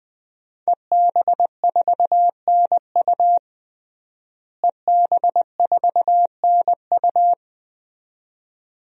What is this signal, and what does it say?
Morse code, heard as a keyed tone, about 20 wpm, 700 Hz. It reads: EB4NU EB4NU
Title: DXCC-100 Call Signs - 5 Characters - Repeated 2x - 20wpm